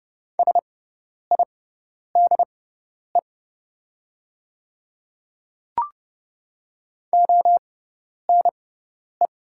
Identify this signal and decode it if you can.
Morse code, heard as a keyed tone, about 29 wpm, spaced as for 10 wpm, 700 Hz. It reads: SIDE ONE
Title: 6x Character Spacing - Top 100 Words - 30wpm